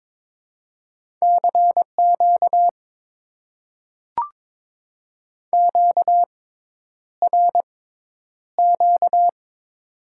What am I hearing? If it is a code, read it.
Morse code, heard as a keyed tone, about 22 wpm, spaced as for 10 wpm, 700 Hz. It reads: CQ QRQ
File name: Top 90 QSO Elements - 6x Character Spacing - 22wpm